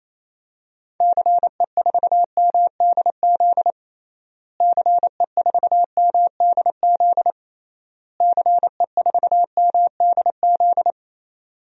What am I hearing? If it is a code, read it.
Morse code, heard as a keyed tone, about 27 wpm, 700 Hz. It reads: CE4MDZ CE4MDZ CE4MDZ
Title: DXCC-100 Call Signs - 6 Characters - Repeated 3x - 28wpm